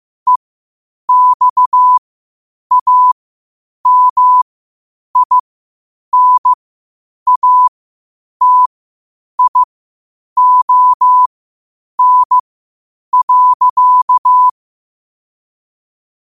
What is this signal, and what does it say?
Morse code, heard as a keyed tone, about 15 wpm, spaced as for 8 wpm, 1000 Hz. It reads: EXAMINATION.
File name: qotdfiles / 8wpm